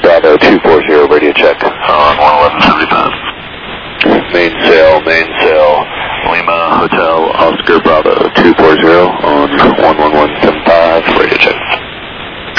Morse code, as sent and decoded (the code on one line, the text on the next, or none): none